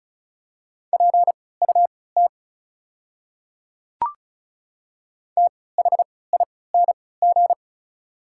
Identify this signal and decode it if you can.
Morse code, heard as a keyed tone, about 34 wpm, spaced as for 20 wpm, 690 Hz. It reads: PUT THING
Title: 3x Character Spacing - Top 300 Words - 35wpm